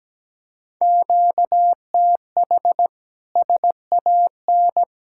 Morse code, heard as a keyed tone, about 17 wpm, 700 Hz. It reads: QTH SAN
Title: Speed-Racing Common QSO Phrases - 17wpm